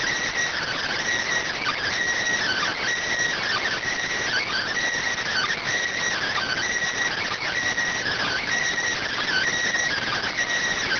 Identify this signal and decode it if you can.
Emergency Alert System (SAME) tones (header bursts), heard as harsh buzzing bursts. It is heard from 0.6 to 10.3 s.